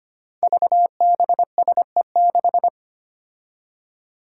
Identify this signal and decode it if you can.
Morse code, heard as a keyed tone, about 25 wpm, 700 Hz. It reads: VBSE6
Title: Five Challenging Letter-Number Combinations 25wpm